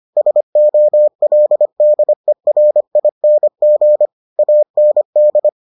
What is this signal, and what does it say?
Morse code, heard as a keyed tone, about 25 wpm, 600 Hz. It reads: SOLDERING AND